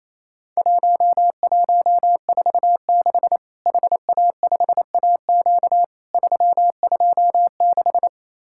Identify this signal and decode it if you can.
Morse code, heard as a keyed tone, about 28 wpm, 700 Hz. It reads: 1146 HA5AQ 326
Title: Worked All Europe DX Contest - QTC Exchange - 28wpm